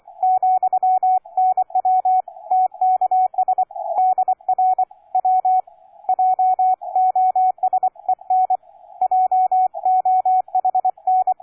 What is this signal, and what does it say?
Morse code, heard as a keyed tone, about 24 wpm, 740 Hz, with radio noise. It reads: , NW TKS DR W J OS EN JOHN
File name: cw forum / K9DX 1818 le 17-12-2008 05h47